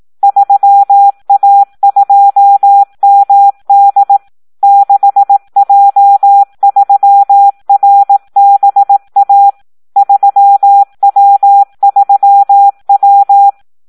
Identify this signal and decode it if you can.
Morse code, heard as a keyed tone, about 18 wpm, 800 Hz. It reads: 3A2MD 6J3RBA 3W3W
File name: Audio practica CW / SD 1 18